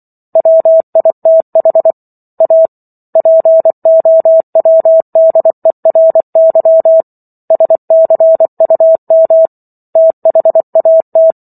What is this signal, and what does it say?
Morse code, heard as a keyed tone, about 24 wpm, 650 Hz. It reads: WITH A POWDERY SCUM THAT